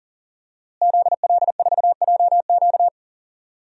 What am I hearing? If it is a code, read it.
Morse code, heard as a keyed tone, about 40 wpm, 700 Hz. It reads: ZL4JQ
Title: DXCC-100 Call Signs - 5 Characters - Repeated 3x - 40wpm